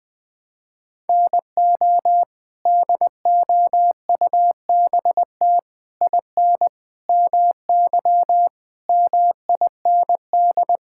Morse code, heard as a keyed tone, about 20 wpm, 700 Hz. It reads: NO DOUBT IN MY MIND
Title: Sets of 6 Words - 20wpm